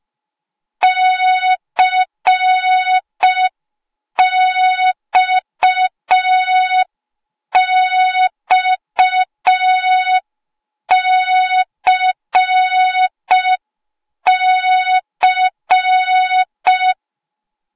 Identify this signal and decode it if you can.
Morse code, heard as a keyed tone, about 5 wpm, 3700 Hz. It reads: CXXCC
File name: Morze-HA2MN / hang / 08KXC HA2MN